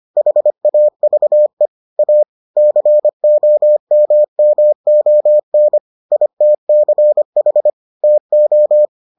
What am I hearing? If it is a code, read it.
Morse code, heard as a keyed tone, about 25 wpm, 600 Hz. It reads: HAVE A COMMON ITCH TO